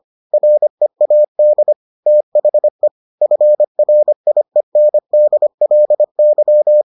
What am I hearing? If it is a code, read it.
Morse code, heard as a keyed tone, about 25 wpm, 600 Hz. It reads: READ THE FRIENDLY